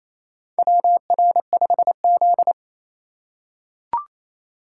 Morse code, heard as a keyed tone, about 28 wpm, 700 Hz. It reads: WR5Z
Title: USA Call Signs - 4 Characters - 3x Character Spacing - 28wpm